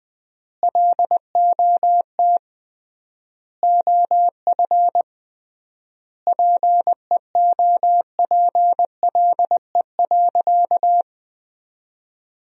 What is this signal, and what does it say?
Morse code, heard as a keyed tone, about 20 wpm, 700 Hz. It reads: LOT OF PEOPLE.
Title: 3x Word-Spacing - Sentences from Top 500 Words - 20wpm